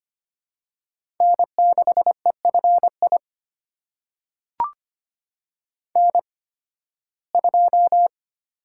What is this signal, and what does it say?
Morse code, heard as a keyed tone, about 25 wpm, spaced as for 16 wpm, 700 Hz. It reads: N6EFI N2
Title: USA Call Signs - 5 Characters - 8x Character Spacing - 25wpm